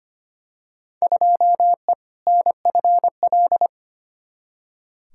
Morse code, heard as a keyed tone, about 25 wpm, 700 Hz. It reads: 2E NFL
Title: Field Day Exchanges Encore - 25wpm